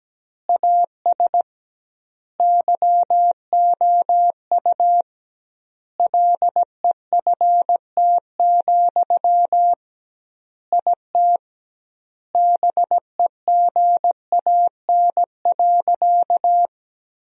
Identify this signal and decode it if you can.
Morse code, heard as a keyed tone, about 17 wpm, 700 Hz. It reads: AS YOU LEFT, IT BEGAN.
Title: 2x Word-Spacing - Sentences from Top 400 Words - 17wpm